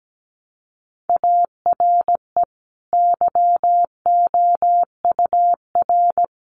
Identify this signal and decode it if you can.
Morse code, heard as a keyed tone, about 17 wpm, 700 Hz. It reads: ARE YOUR
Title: Sentences from Top 1200 Words 17wpm